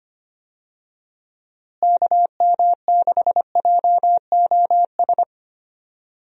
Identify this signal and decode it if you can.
Morse code, heard as a keyed tone, about 25 wpm, 700 Hz. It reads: KM6JOS
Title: USA Call Signs - 6 Characters - Repeated 3x - 25wpm